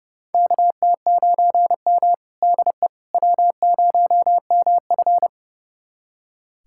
Morse code, heard as a keyed tone, about 30 wpm, 700 Hz. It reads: KT9M DE W0MF
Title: USA Callsign DE Callsign - 4 Characters - 30wpm